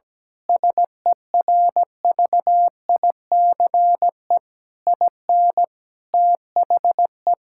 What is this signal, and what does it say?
Morse code, heard as a keyed tone, about 17 wpm, 700 Hz. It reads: SERVICE IN THE
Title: Sentences from Top 1000 Words - Review Part 6 - 17wpm